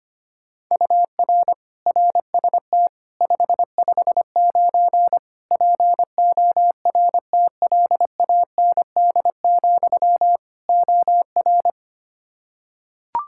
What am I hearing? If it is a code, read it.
Morse code, heard as a keyed tone, about 25 wpm, 700 Hz. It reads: UR RST 559 PORTLAND, OR
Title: K1USN SST QSO Practice 25wpm